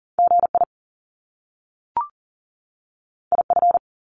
Morse code, heard as a keyed tone, about 38 wpm, 700 Hz. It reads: GI IF
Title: Top Two Letter Combinations in Words 40wpm